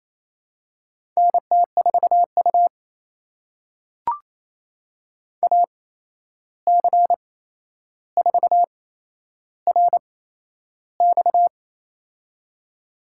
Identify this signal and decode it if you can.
Morse code, heard as a keyed tone, about 28 wpm, spaced as for 11 wpm, 700 Hz. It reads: NT4U AC4RX
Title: USA Call Signs - 8x Character Spacing - 28wpm